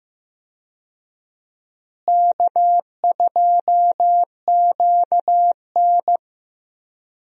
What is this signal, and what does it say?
Morse code, heard as a keyed tone, about 16 wpm, 700 Hz. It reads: K2QN
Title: USA Call Signs - 4 Characters - Repeated 2x - 15wpm